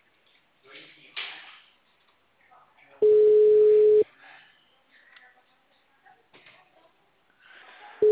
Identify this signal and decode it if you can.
Ringback tone (Europe, 425 Hz), with speech.
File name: out- -2012- - - .1881